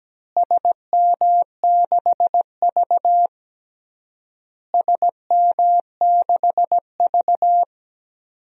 Morse code, heard as a keyed tone, about 17 wpm, 700 Hz. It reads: SM6V SM6V
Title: DXCC-100 Call Signs - 4 Characters - Repeated 3x - 17wpm